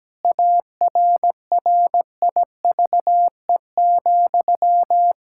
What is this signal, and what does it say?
Morse code, heard as a keyed tone, about 17 wpm, 700 Hz. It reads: ARRIVE,
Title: Pangram Challenge v2 - 17wpm